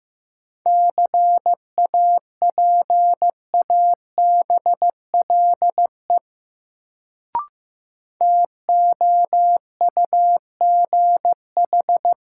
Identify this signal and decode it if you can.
Morse code, heard as a keyed tone, about 15 wpm, 700 Hz. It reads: CAPABLE TOUGH
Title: Sentences from Top 1200 Words - New Word Drill Alt - 15wpm